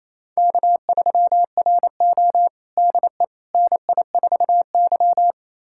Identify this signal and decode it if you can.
Morse code, heard as a keyed tone, about 28 wpm, 700 Hz. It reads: K3RO DE NI4Y
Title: USA Callsign DE Callsign - 4 Characters - 28wpm